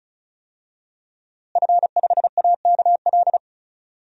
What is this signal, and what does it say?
Morse code, heard as a keyed tone, about 36 wpm, 700 Hz. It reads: F5AKL
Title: DXCC-100 Call Signs - 5 Characters - Repeated 2x - 35wpm